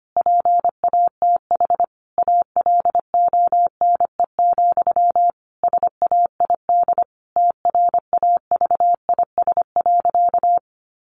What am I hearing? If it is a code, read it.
Morse code, heard as a keyed tone, about 25 wpm, 700 Hz. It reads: PATH ALONE, SAID TRAVIS.
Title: A Sound Of Thunder 25wpm